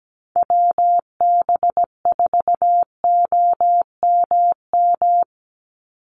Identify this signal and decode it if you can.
Morse code, heard as a keyed tone, about 17 wpm, 700 Hz. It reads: WB4OMM
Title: Call Signs Repeated 3x - 17wpm